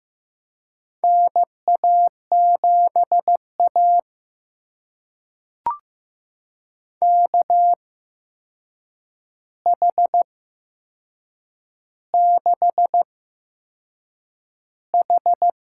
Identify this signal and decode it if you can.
Morse code, heard as a keyed tone, about 15 wpm, spaced as for 6 wpm, 700 Hz. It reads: NA7A KH6H
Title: USA Call Signs - 4 Characters - 8x Character Spacing - 15wpm